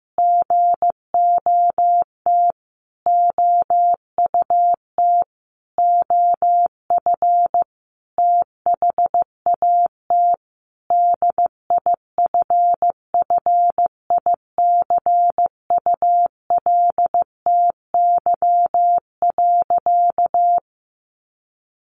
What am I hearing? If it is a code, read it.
Morse code, heard as a keyed tone, about 15 wpm, 700 Hz. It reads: GOT OUT OF THAT DIFFICULTY.
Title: Sentences from Top 1500 Words 15wpm